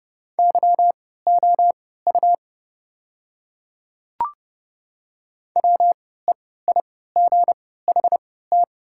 Morse code, heard as a keyed tone, about 30 wpm, spaced as for 17 wpm, 700 Hz. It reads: YOU WEIGHT
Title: 3x Character Spacing - Top 500 Words - 30wpm